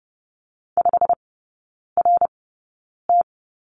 Morse code, heard as a keyed tone, about 30 wpm, spaced as for 9 wpm, 700 Hz. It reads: 5RT
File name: course / 30:10wpm / 087 Lesson 15e - Number 5 - Three Characters 30wpm